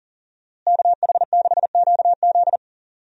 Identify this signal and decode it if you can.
Morse code, heard as a keyed tone, about 38 wpm, 700 Hz. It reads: KH6QZ
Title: USA Call Signs - 5 Characters - Repeated 2x - 40wpm